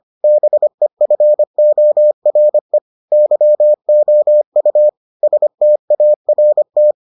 Morse code, heard as a keyed tone, about 25 wpm, 600 Hz. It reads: BEFORE YOU START